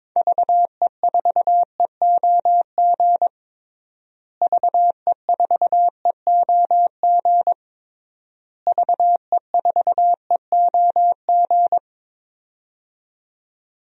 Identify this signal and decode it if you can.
Morse code, heard as a keyed tone, about 22 wpm, 700 Hz. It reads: VE4EOG VE4EOG VE4EOG
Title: DXCC-100 Call Signs - 6 Characters - Repeated 3x - 22wpm